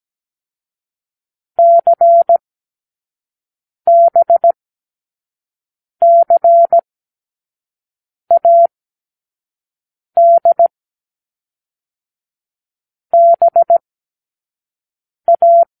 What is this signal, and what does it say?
Morse code, heard as a keyed tone, about 17 wpm, 680 Hz. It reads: C B C A D B A
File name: k7qo / Lesson6(A-D)